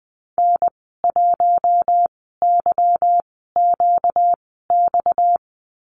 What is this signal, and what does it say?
Morse code, heard as a keyed tone, about 20 wpm, spaced as for 15 wpm, 700 Hz. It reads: N1YQX
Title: 238 Lesson 38g - Letter q - Call Signs 20wpm